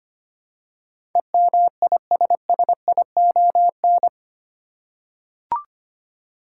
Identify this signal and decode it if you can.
Morse code, heard as a keyed tone, about 25 wpm, 700 Hz. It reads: EMISSION
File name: Eight Letter Words - 25wpm